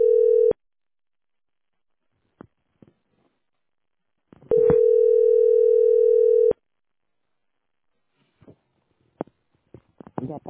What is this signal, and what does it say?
Ringback tone (North America), 440 + 480 Hz.